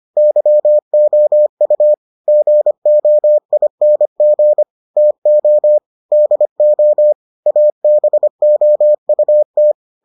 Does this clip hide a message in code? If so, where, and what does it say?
Yes: Morse code from 0.2 to 9.7 s, heard as a keyed tone, about 25 wpm, 600 Hz. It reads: YOU GOING TO DO ABOUT